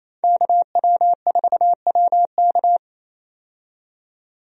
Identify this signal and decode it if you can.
Morse code, heard as a keyed tone, about 27 wpm, 700 Hz. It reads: KW4WK
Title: USA Call Signs - 5 Characters - 28wpm